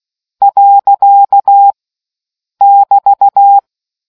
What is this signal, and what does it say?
Morse code, heard as a keyed tone, about 16 wpm, 800 Hz. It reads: . <BT>